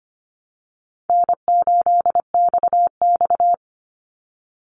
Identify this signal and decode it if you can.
Morse code, heard as a keyed tone, about 25 wpm, 700 Hz. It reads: N8XX
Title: Easy Call Signs Repeated 2x - 25wpm